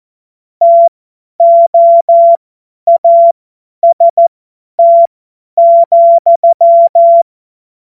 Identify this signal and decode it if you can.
Morse code, heard as a keyed tone, about 14 wpm, spaced as for 10 wpm, 680 Hz. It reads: TOAST,
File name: Morsekurs / Witze / Witz 26 März 2024 50 F70